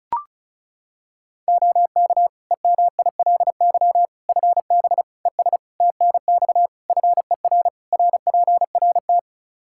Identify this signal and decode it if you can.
Morse code, heard as a keyed tone, about 34 wpm, 700 Hz. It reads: OK EMILY FB ES TNX FER RPRT
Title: Practice QSO - Exchange 2 - Info - 35wpm